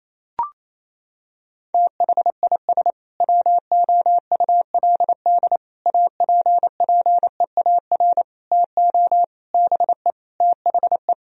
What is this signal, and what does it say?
Morse code, heard as a keyed tone, about 28 wpm, 700 Hz. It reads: THIS WOULD APPEAR TO BE THE
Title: Sentences from Top 1000 Words - Review Part 3 - 28wpm